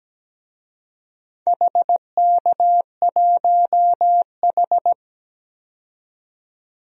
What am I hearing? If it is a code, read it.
Morse code, heard as a keyed tone, about 17 wpm, 700 Hz. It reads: HK1H
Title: DXCC-100 Call Signs - 4 Characters - 17wpm